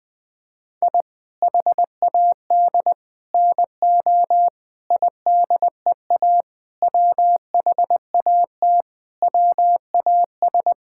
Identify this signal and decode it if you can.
Morse code, heard as a keyed tone, about 20 wpm, 700 Hz. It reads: I HAD NO IDEA WHAT WAS
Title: Sets of 8 Words - 20wpm